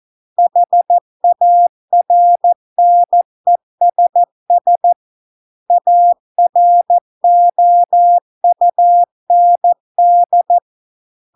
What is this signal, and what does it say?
Morse code, heard as a keyed tone, about 14 wpm, 700 Hz. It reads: HARNESS AROUND